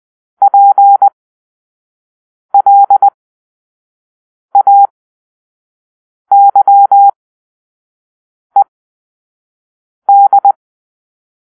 Morse code, heard as a keyed tone, about 20 wpm, spaced as for 5 wpm, 800 Hz. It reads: PLAYED
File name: Grp-5 Drill-2